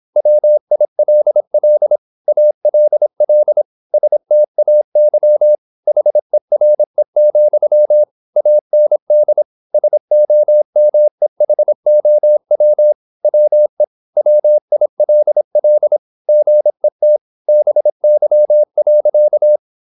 Morse code, heard as a keyed tone, about 26 wpm, 600 Hz. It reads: WILL ALL STAY HERE, AND SOMEHOW WE WILL GET BY.
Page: Never 26wpm | CW med Gnister